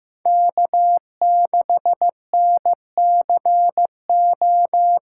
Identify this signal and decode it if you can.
Morse code, heard as a keyed tone, about 16 wpm, 700 Hz. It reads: K6NCO
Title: USA Call Signs - 5 Characters - 4x Character Spacing - 15wpm